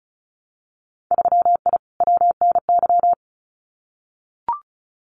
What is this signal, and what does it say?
Morse code, heard as a keyed tone, about 34 wpm, 700 Hz. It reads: 3I WNY
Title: Winter Field Day Exchanges 35wpm